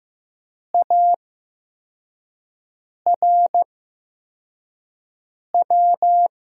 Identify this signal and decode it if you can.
Morse code, heard as a keyed tone, about 15 wpm, spaced as for 4 wpm, 700 Hz. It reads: ARW
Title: USA Call Signs - 8x Character Spacing - 15wpm